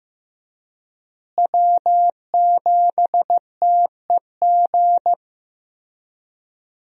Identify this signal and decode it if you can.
Morse code, heard as a keyed tone, about 16 wpm, 700 Hz. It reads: W7TEG
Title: USA Call Signs - 5 Characters - Repeated 3x - 15wpm